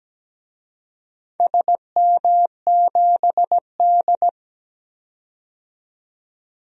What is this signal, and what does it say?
Morse code, heard as a keyed tone, about 17 wpm, 700 Hz. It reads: SM7D
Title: DXCC-100 Call Signs - 4 Characters - 17wpm